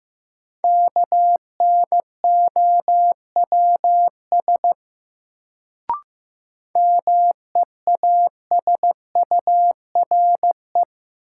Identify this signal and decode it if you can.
Morse code, heard as a keyed tone, about 15 wpm, 700 Hz. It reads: KNOWS MEASURE
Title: Sentences from Top 1000 Words - Review Part 1 - New Word Drill - 15wpm